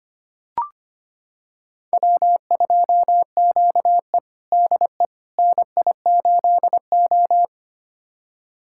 Morse code, heard as a keyed tone, about 25 wpm, 700 Hz. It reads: W2QE DE NI8O
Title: USA Callsign DE Callsign - 4 Characters - 25wpm